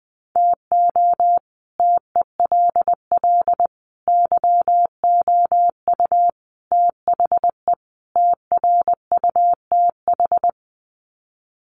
Morse code, heard as a keyed tone, about 20 wpm, 700 Hz. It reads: TO TELL YOU THE TRUTH
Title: Sets of 5 Words 20wpm